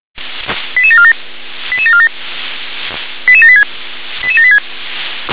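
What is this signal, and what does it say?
Five-tone selective call (ZVEI), heard as quick runs of five tones, tones of about 70 ms. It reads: 88747 88747 88786 88786
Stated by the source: ZVEI five-tone call received by SDR.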